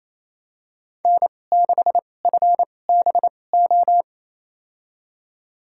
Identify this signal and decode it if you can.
Morse code, heard as a keyed tone, about 28 wpm, spaced as for 20 wpm, 700 Hz. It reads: N6FBO
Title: USA Call Signs - 5 Characters - 2x Character Spacing - 28wpm